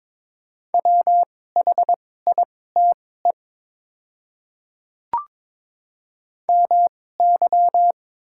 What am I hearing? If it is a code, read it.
Morse code, heard as a keyed tone, about 22 wpm, spaced as for 16 wpm, 700 Hz. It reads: WHITE MY
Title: 2x Character Spacing - Top 300 Words - 22wpm